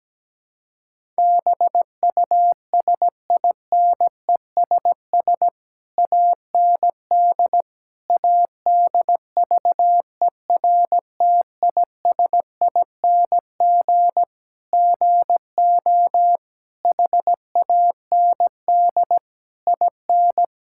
Morse code, heard as a keyed tone, about 17 wpm, 700 Hz. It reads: BUSINESS AND ADVERTISING GO HAND IN
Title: Sentences from Top 1000 Words - Review Part 6 - 17wpm